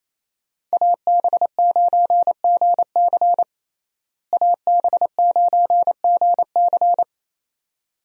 Morse code, heard as a keyed tone, about 28 wpm, 700 Hz. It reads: AB9GC AB9GC
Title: USA Call Signs - 5 Characters - Repeated 2x - 28wpm